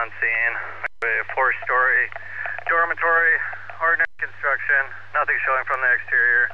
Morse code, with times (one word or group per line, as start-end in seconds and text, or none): none